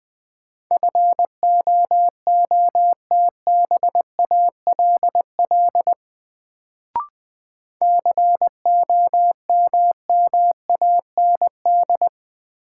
Morse code, heard as a keyed tone, about 20 wpm, 700 Hz. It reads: FOOTBALL COMMAND
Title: Sentences from Top 1600 Words - New Word Drill - 20wpm